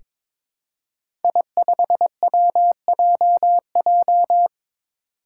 Morse code, heard as a keyed tone, about 22 wpm, 700 Hz. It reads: I5WJJ
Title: Five Challenging Letter-Number Combinations 22wpm